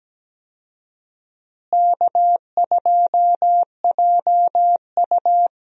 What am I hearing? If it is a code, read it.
Morse code, heard as a keyed tone, about 17 wpm, 700 Hz. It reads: K2JU
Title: USA Call Signs - 4 Characters - Repeated 2x - 17wpm